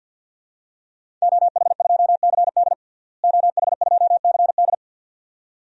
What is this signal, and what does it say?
Morse code, heard as a keyed tone, about 48 wpm, 700 Hz. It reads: OH2CD OH2CD
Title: DXCC-100 Call Signs - 5 Characters - Repeated 2x - 50wpm